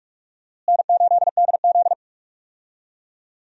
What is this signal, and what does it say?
Morse code, heard as a keyed tone, about 46 wpm, 700 Hz. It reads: N8DZ
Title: USA Call Signs - 4 Characters - Repeated 2x - 45wpm